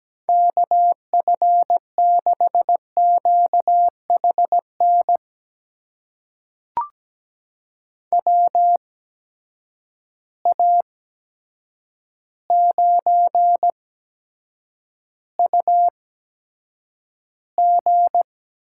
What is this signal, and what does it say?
Morse code, heard as a keyed tone, about 17 wpm, spaced as for 8 wpm, 700 Hz. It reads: KF6QHN WA9UG
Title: USA Call Signs - 8x Character Spacing - 17wpm